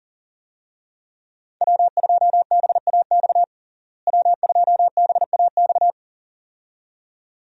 Morse code, heard as a keyed tone, about 39 wpm, 700 Hz. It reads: W2BAX W2BAX
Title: USA Call Signs - 5 Characters - Repeated 2x - 40wpm